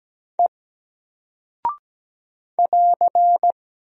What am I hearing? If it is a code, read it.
Morse code, heard as a keyed tone, about 17 wpm, 700 Hz. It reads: E <AR>
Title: Single Character (Letter:Number:Punctuation:Prosign) Alt 17wpm